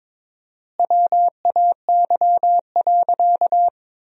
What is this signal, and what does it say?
Morse code, heard as a keyed tone, about 22 wpm, 700 Hz. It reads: WAY.
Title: 4x Word-Spacing - Sentences from Top 200 Words - 22wpm